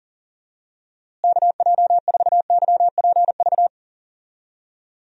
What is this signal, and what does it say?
Morse code, heard as a keyed tone, about 40 wpm, 700 Hz. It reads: KJ4YPV
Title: USA Call Signs - 6 Characters - Repeated 3x - 40wpm